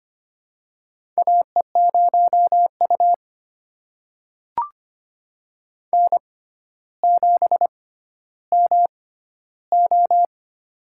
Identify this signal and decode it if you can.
Morse code, heard as a keyed tone, about 25 wpm, spaced as for 13 wpm, 700 Hz. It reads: AE0U N7MO
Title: USA Call Signs - 4 Characters - 6x Character Spacing - 25wpm